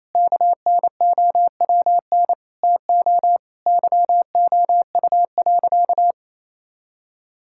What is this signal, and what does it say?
Morse code, heard as a keyed tone, about 27 wpm, 700 Hz. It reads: KNOWN TO YOU.
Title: Sentences from Top 1000 Words - Review Part 5 - 28wpm